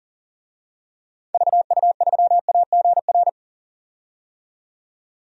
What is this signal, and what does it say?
Morse code, heard as a keyed tone, about 37 wpm, 700 Hz. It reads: VU3AGR
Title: DXCC-100 Call Signs - 6 Characters - 40wpm